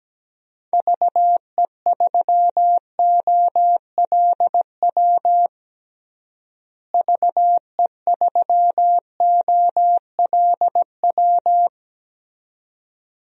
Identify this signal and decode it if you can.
Morse code, heard as a keyed tone, about 17 wpm, 700 Hz. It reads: VE3OLW VE3OLW
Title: DXCC-100 Call Signs - 6 Characters - Repeated 2x - 17wpm